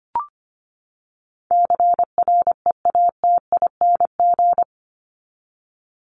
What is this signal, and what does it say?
Morse code, heard as a keyed tone, about 25 wpm, 700 Hz. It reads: CREATING
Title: ING Suffix - Top 100 Words with Base Words 25wpm